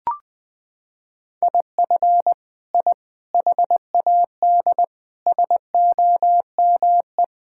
Morse code, heard as a keyed tone, about 20 wpm, 700 Hz. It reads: IF I HAD SOME
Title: Sentences from Top 1000 Words - Review Part 5 - 20wpm